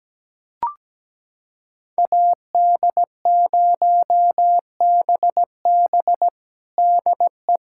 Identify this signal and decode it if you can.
Morse code, heard as a keyed tone, about 17 wpm, 700 Hz. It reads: AD0BB DE
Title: USA Callsign DE Callsign - 5 Characters - 17wpm